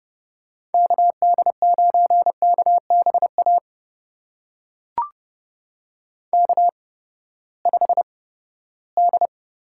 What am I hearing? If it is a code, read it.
Morse code, heard as a keyed tone, about 30 wpm, spaced as for 17 wpm, 700 Hz. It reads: KD9KBA K5D
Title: USA Call Signs - 8x Character Spacing - 30wpm